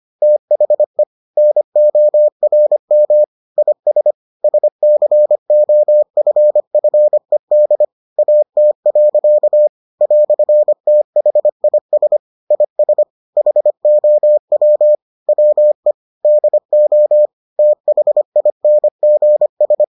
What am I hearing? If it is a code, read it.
Morse code, heard as a keyed tone, about 25 wpm, 600 Hz. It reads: THE NORM IS SCOFFED AT. "THIS IS HOW WE DO THINGS